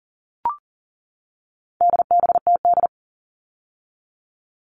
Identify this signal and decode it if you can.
Morse code, heard as a keyed tone, about 40 wpm, 700 Hz. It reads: DBTD
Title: All Cut Numbers - 4-Digit 40wpm